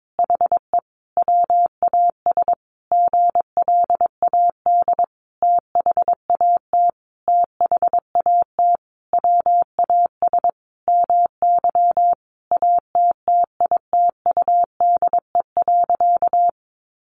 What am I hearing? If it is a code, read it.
Morse code, heard as a keyed tone, about 22 wpm, 700 Hz. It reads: HE WAS GLAD THAT THAT WAS MY ATTITUDE.
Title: Sentences from Top 1300 Words 22wpm